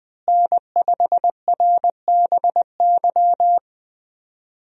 Morse code, heard as a keyed tone, about 20 wpm, 700 Hz. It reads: N5RBY
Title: USA Call Signs - 5 Characters - Repeated 3x - 20wpm